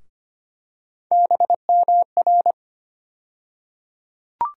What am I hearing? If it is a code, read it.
Morse code, heard as a keyed tone, about 25 wpm, 700 Hz. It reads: BMR
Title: USA Call Sign Postfixes - 3 Letters 25wpm